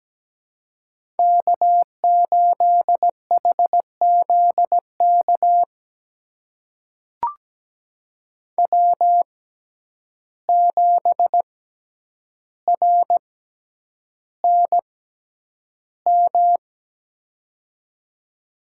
Morse code, heard as a keyed tone, about 17 wpm, spaced as for 9 wpm, 700 Hz. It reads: K8HZK W7RNM
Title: USA Call Signs - 5 Characters - 6x Character Spacing - 17wpm